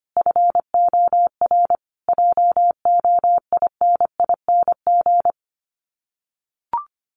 Morse code, heard as a keyed tone, about 25 wpm, 700 Hz. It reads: FOR JOINING
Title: Sets of 5 Words Encore2 25wpm